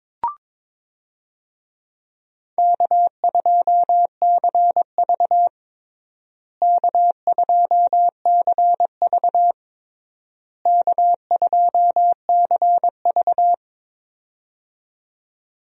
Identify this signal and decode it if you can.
Morse code, heard as a keyed tone, about 22 wpm, 700 Hz. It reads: K2CV K2CV K2CV
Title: USA Call Signs - 4 Characters - Repeated 3x - 22wpm